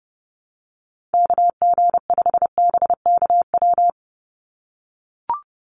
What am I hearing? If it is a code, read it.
Morse code, heard as a keyed tone, about 30 wpm, 700 Hz. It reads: KG5BKW
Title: USA Call Signs 30wpm